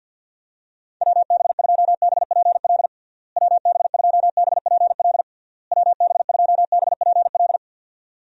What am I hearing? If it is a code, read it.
Morse code, heard as a keyed tone, about 52 wpm, 700 Hz. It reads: WB2BPL WB2BPL WB2BPL
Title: USA Call Signs - 6 Characters - Repeated 3x - 50wpm